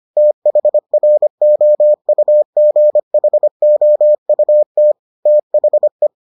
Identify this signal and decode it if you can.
Morse code, heard as a keyed tone, about 25 wpm, 600 Hz. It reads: THROUGHOUT THE